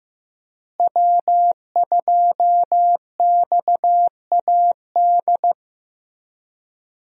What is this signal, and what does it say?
Morse code, heard as a keyed tone, about 15 wpm, 700 Hz. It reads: W2XAD
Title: USA Call Signs - 5 Characters - Repeated 3x - 15wpm